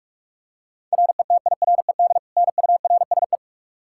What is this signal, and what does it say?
Morse code, heard as a keyed tone, about 46 wpm, 700 Hz. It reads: RETIRED NURSE
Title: Practice QSO - Exchange 3 - More Info - 45wpm